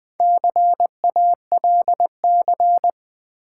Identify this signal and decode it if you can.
Morse code, heard as a keyed tone, about 20 wpm, 700 Hz. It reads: CALC
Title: Build A Word - 4 Letters At A Time - 11-Letter Words - 20wpm